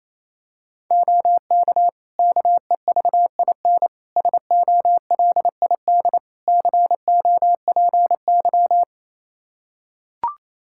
Morse code, heard as a keyed tone, about 28 wpm, 700 Hz. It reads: OK KEVIN SOLID COPY
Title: Practice QSO - Exchange 3 - More Info - 28wpm